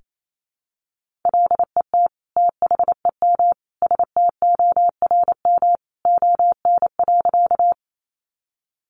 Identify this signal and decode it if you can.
Morse code, heard as a keyed tone, about 28 wpm, 700 Hz. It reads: LET THEM STORM ON.
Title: Sentences from Top 1500 Words 28wpm